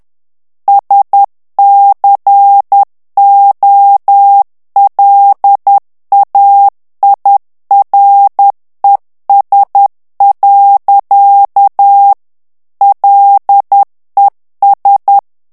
Morse code, heard as a keyed tone, about 10 wpm, 800 Hz. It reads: SCOLAIRES. LES
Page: Français - texte en clair 1 - vitesse : 10 mots minute